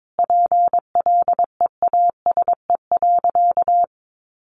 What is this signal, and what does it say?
Morse code, heard as a keyed tone, about 22 wpm, 700 Hz. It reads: PLEASE.
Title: Sentences from Top 1500 Words 22wpm